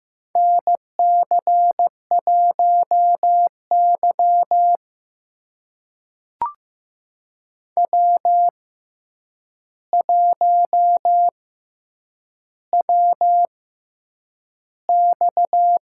Morse code, heard as a keyed tone, about 15 wpm, spaced as for 8 wpm, 700 Hz. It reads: NC1Y W1WX
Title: USA Call Signs - 4 Characters - 6x Character Spacing - 15wpm